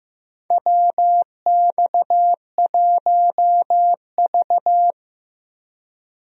Morse code, heard as a keyed tone, about 16 wpm, 700 Hz. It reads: WX1V
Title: USA Call Signs - 4 Characters - Repeated 3x - 15wpm